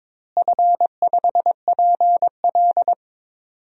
Morse code, heard as a keyed tone, about 22 wpm, 700 Hz. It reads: F5PL
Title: DXCC-100 Call Signs - 4 Characters - Repeated 2x - 22wpm